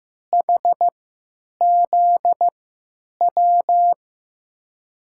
Morse code, heard as a keyed tone, about 16 wpm, spaced as for 9 wpm, 700 Hz. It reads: HZW
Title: USA Call Signs - 6 Characters - 3x Character Spacing - 15wpm